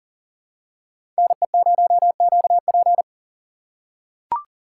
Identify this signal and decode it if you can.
Morse code, heard as a keyed tone, about 40 wpm, 700 Hz. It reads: NE0QP
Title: USA Call Signs - 4x Character Spacing - 40wpm